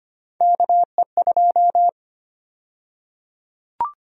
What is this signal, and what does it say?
Morse code, heard as a keyed tone, about 25 wpm, 700 Hz. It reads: KE2
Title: USA Call Sign Prefixes - 25wpm